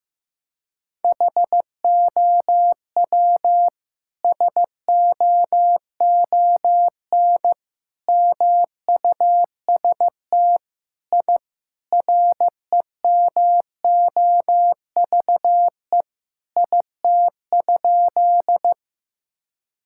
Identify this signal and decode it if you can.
Morse code, heard as a keyed tone, about 15 wpm, 700 Hz. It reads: HOW SOON MUST I REMOVE IT?
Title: Sentences from Top 1000 Words - Review Part 5 - 15wpm